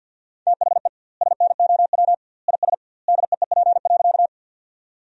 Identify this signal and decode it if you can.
Morse code, heard as a keyed tone, about 50 wpm, 700 Hz. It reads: THE SNOW IS DEEP.
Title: 2x Word-Spacing - Sentences from Top 500 Words - 50wpm